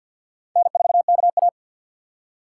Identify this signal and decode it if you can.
Morse code, heard as a keyed tone, about 64 wpm, 700 Hz. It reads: N4KA